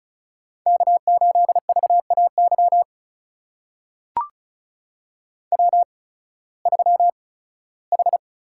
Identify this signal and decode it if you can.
Morse code, heard as a keyed tone, about 36 wpm, spaced as for 19 wpm, 700 Hz. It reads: K8VAY W3H
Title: USA Call Signs - 5 Characters - 8x Character Spacing - 35wpm